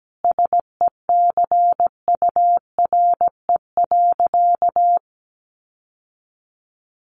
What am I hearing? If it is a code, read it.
Morse code, heard as a keyed tone, about 17 wpm, 700 Hz. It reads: SECURE.
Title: Sentences from Top 1200 Words 17wpm